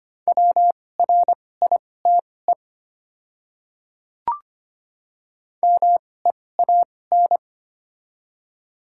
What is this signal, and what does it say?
Morse code, heard as a keyed tone, about 25 wpm, spaced as for 18 wpm, 700 Hz. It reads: WRITE MEAN
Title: 2x Character Spacing - Top 200 Words - 25wpm